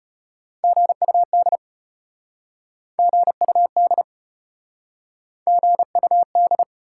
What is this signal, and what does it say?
Morse code, heard as a keyed tone, about 34 wpm, 700 Hz. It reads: GUD GUD GUD